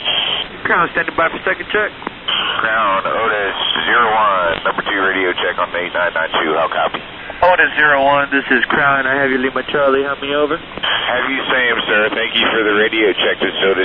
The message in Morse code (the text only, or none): none